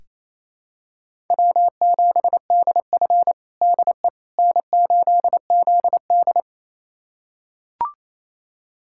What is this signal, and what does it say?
Morse code, heard as a keyed tone, about 28 wpm, 700 Hz. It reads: W7DF DE N8ZD
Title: USA Callsign DE Callsign - 4 Characters - 28wpm